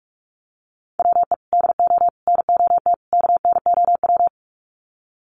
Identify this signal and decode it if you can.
Morse code, heard as a keyed tone, about 48 wpm, 700 Hz. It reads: WE DO NOT KNOW
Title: Sets of 4 Words 45wpm